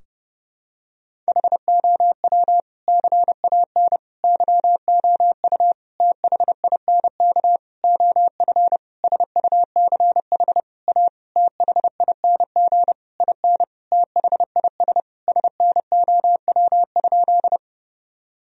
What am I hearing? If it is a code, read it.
Morse code, heard as a keyed tone, about 30 wpm, 700 Hz. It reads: HOW CAN YOU THINK OF SUCH A THING IN THIS SNOW?
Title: Sentences from Top 1000 Words - Review Part 3 - 30wpm